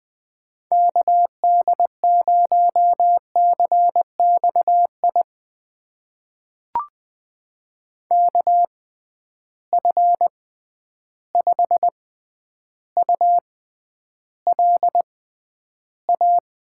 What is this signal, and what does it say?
Morse code, heard as a keyed tone, about 20 wpm, spaced as for 10 wpm, 700 Hz. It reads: KD0CXI KF5ULA
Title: USA Call Signs - 6 Characters - 6x Character Spacing - 20wpm